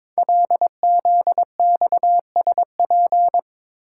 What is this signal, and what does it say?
Morse code, heard as a keyed tone, about 22 wpm, 700 Hz. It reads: LZXSP
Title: Code Groups - 5 Letters - 22wpm